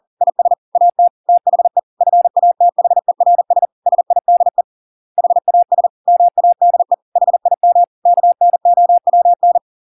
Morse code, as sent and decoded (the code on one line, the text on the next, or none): .. ... / .- - / - .... / ..-. .- - .... .-. ... / ... .. -.. / .... .- ... / -- .- -.. / .... .. -- / -.- -. --- .-- -.
IS AT THE FATHERS SIDE HAS MADE HIM KNOWN